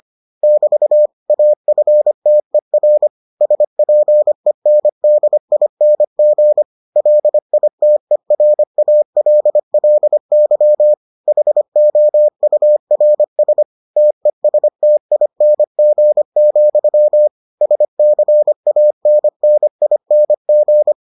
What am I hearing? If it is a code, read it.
Morse code, heard as a keyed tone, about 25 wpm, 600 Hz. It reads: <BT> AFTER SPENDING LITERALLY HOURS TESTING, SCANNING